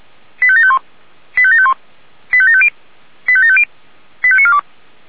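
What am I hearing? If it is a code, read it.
Five-tone selective call (ZVEI), heard as quick runs of five tones, tones of about 70 ms. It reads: 85751 85751 85750 85750 85942
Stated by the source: ZVEI five-tone selective call 85730 received on an SDR.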